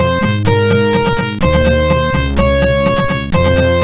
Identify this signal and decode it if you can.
There is no code.